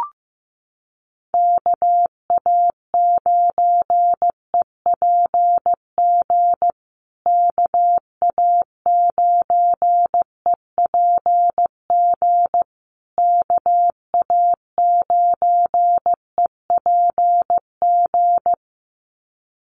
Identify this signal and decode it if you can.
Morse code, heard as a keyed tone, about 15 wpm, 700 Hz. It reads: KA9EPG KA9EPG KA9EPG
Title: USA Call Signs Repeated 3x 15wpm